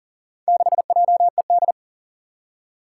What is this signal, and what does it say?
Morse code, heard as a keyed tone, about 38 wpm, 700 Hz. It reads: 6JED
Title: Four Challenging Letter-Number Combinations 40wpm